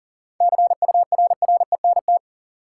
Morse code, heard as a keyed tone, about 40 wpm, 700 Hz. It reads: CURRENT
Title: Sentences from Top 1600 Words - New Word Drill Alt - 40wpm